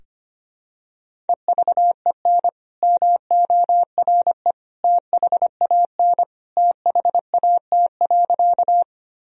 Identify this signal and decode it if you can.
Morse code, heard as a keyed tone, about 25 wpm, 700 Hz. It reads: EVEN MORE THAN THAT.
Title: Sentences from Top 1000 Words - Review Part 1 - 25wpm